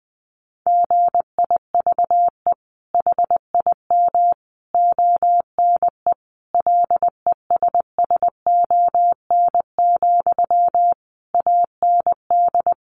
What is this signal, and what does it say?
Morse code, heard as a keyed tone, about 20 wpm, 700 Hz. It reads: GIVE HIM ONE LESSON, AND
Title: Sentences from Top 1200 Words 20wpm